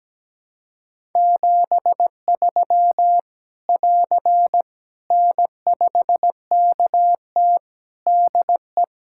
Morse code, heard as a keyed tone, about 17 wpm, 700 Hz. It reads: 73 <AR> N5KT DE
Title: Practice QSO - Exchange 4 - Ending - 17wpm